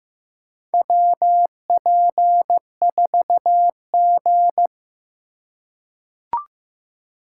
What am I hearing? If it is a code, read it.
Morse code, heard as a keyed tone, about 15 wpm, 700 Hz. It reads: WP4G
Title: USA Call Signs - 4 Characters - 2x Character Spacing - 15wpm